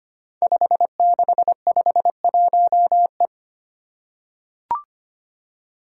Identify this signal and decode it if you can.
Morse code, heard as a keyed tone, about 25 wpm, 700 Hz. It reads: 5651E
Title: Five Challenging Letter-Number Combinations 25wpm